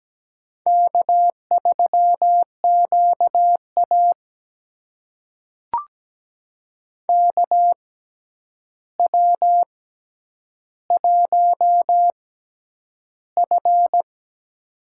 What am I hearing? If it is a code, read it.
Morse code, heard as a keyed tone, about 17 wpm, spaced as for 9 wpm, 700 Hz. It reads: K3QA KW1F
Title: USA Call Signs - 4 Characters - 6x Character Spacing - 17wpm